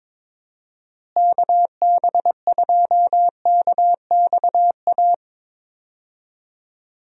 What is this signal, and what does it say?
Morse code, heard as a keyed tone, about 22 wpm, 700 Hz. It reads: KB2KXA
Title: USA Call Signs - 6 Characters - Repeated 3x - 22wpm